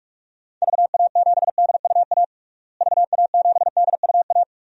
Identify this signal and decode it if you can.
Morse code, heard as a keyed tone, about 46 wpm, 700 Hz. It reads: VA7DUA VA7DUA
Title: DXCC-100 Call Signs - 6 Characters - Repeated 2x - 45wpm